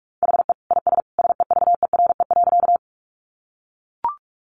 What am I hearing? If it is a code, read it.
Morse code, heard as a keyed tone, about 44 wpm, 700 Hz. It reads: HE IS SEVERE.
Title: Sentences from Top 1400 Words 45wpm